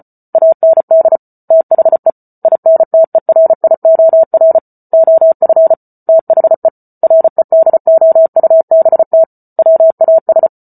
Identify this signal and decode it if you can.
Morse code, heard as a keyed tone, about 34 wpm, 650 Hz. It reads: AND THE INTERIOR OF THE REDOUBT WAS